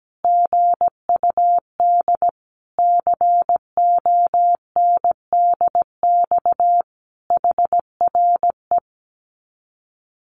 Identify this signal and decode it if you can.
Morse code, heard as a keyed tone, about 17 wpm, 700 Hz. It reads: GUD CONDX HRE
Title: Common QSO Phrases 17wpm